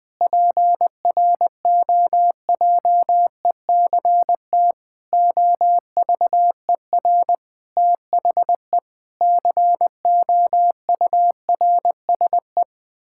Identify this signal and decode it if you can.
Morse code, heard as a keyed tone, about 20 wpm, 700 Hz. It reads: PROJECT OVER THE COURSE
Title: Sentences with 6-Word Phrases - 20wpm